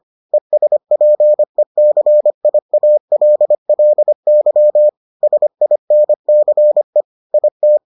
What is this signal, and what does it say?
Morse code, heard as a keyed tone, about 25 wpm, 600 Hz. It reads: ESPECIALLY SINCE IT